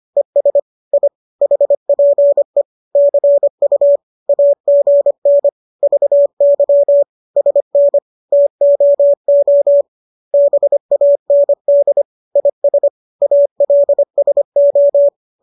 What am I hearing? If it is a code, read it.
Morse code, heard as a keyed tone, about 25 wpm, 570 Hz. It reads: ES I HPE CU AGN VY SN TOO BAND IS ALSO